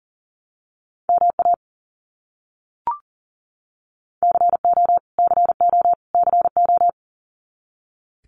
Morse code, heard as a keyed tone, about 40 wpm, 700 Hz. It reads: MA CO CO CO
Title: US States Repeated 3x 40wpm